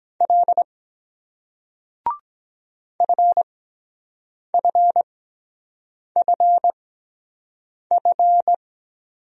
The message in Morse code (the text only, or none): L FFFF